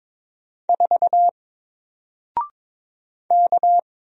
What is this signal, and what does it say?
Morse code, heard as a keyed tone, about 22 wpm, 700 Hz. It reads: 4 K
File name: Single-Letter-Number-22wpm